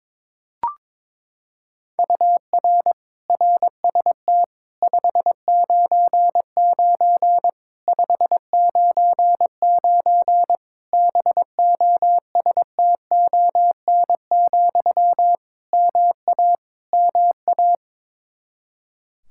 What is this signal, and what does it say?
Morse code, heard as a keyed tone, about 22 wpm, 700 Hz. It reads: UR RST 599 599 BOSTON, MA MA
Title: Practice QSO - Exchange 1 - Beginning - 22wpm